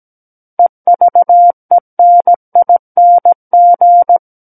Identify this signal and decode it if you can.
Morse code, heard as a keyed tone, about 17 wpm, 700 Hz. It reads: EVENING
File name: Book One / 08-Chapter 8 - 17 WPM